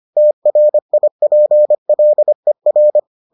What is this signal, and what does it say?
Morse code, heard as a keyed tone, about 25 wpm, 600 Hz. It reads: TRIPLER